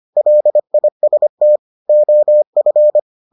Morse code, heard as a keyed tone, about 25 wpm, 600 Hz. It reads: LIST OF